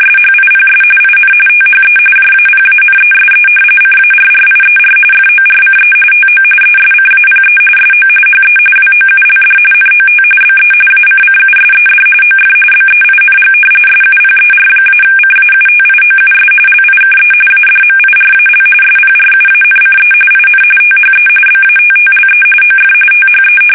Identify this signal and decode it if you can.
RTTY, about 45.45 baud. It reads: SHIPS IN THE JAPANESE CARRIER FORCE TO ESCAPE SO HE WOULD NOT BE ACCUSED OF BEING SUPER-CAUTIOUS LIKE ADMIRAL SPRUANCE IN THE BATTLE OF THE
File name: fsk850 / battleshipnj